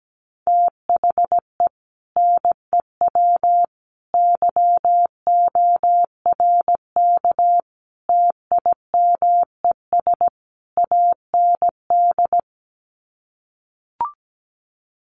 Morse code, heard as a keyed tone, about 17 wpm, 700 Hz. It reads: THE NEW YORK TIMES AND
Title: Sets of 5 Words Encore 17wpm